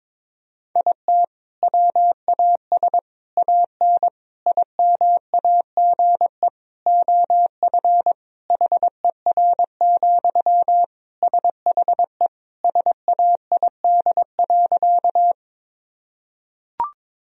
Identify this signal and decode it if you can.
Morse code, heard as a keyed tone, about 22 wpm, 700 Hz. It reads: IT WAS AN IMAGE OF HER, SHE SAID.
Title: Sentences from Top 1000 Words - Review Part 4 - 22wpm